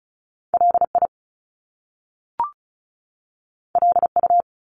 Morse code, heard as a keyed tone, about 36 wpm, 700 Hz. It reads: LI LU
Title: Top Two Letter Combinations in Words 35wpm